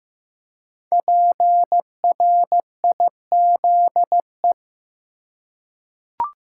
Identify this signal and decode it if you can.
Morse code, heard as a keyed tone, about 15 wpm, 700 Hz. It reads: PRIZE
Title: Five Letter Words - 15wpm